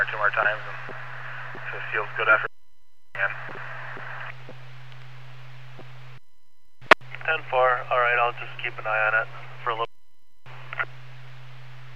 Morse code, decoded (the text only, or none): none